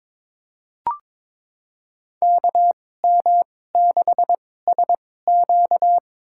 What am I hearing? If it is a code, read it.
Morse code, heard as a keyed tone, about 21 wpm, spaced as for 16 wpm, 700 Hz. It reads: KM6SQ
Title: USA Call Signs - 2x Character Spacing - 22wpm